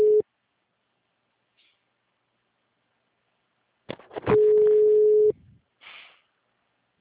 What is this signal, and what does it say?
Ringback tone (Europe, 425 Hz).
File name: out- -2017- - - .43821